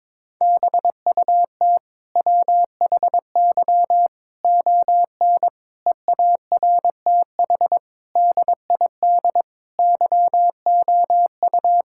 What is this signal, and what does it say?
Morse code, heard as a keyed tone, about 22 wpm, 700 Hz. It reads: BUT WHY ON EARTH DID YOU
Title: Sentences from Top 1000 Words - Review Part 6 - 22wpm